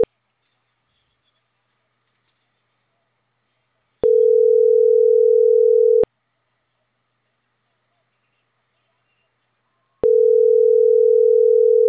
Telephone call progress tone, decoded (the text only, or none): ringback tone